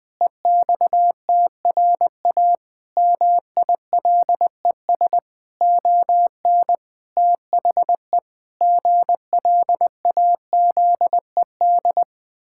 Morse code, heard as a keyed tone, about 21 wpm, 700 Hz. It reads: EXTRA MILES ON THE GLAZED
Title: Pangram Challenge v2 - 20wpm